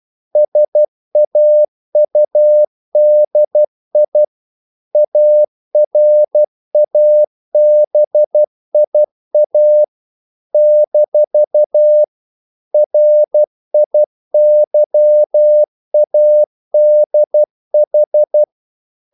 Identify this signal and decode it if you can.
Morse code, heard as a keyed tone, about 12 wpm, 600 Hz. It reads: SAUDI ARABIA - RIYADH
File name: CW / Practice / NationalCapitals12